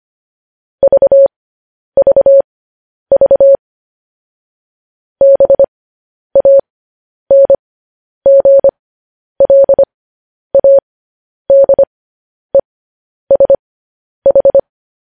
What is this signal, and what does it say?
Morse code, heard as a keyed tone, about 25 wpm, spaced as for 10 wpm, 550 Hz. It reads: VVV BANGLADESH
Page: news headlines in morse code at 10 wpm